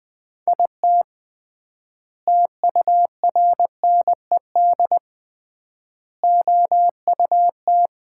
Morse code, heard as a keyed tone, about 20 wpm, 700 Hz. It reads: IT TURNED OUT
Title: 3x Word-Spacing - Sets of 5 Words - 20wpm